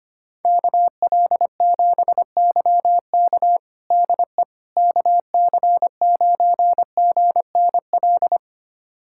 Morse code, heard as a keyed tone, about 25 wpm, 700 Hz. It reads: KL7YK DE KC9GNL
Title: USA Callsign DE Callsign 25wpm